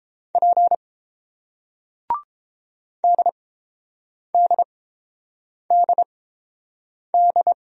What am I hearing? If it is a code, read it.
Morse code, heard as a keyed tone, about 30 wpm, spaced as for 7 wpm, 700 Hz. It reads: P DDDD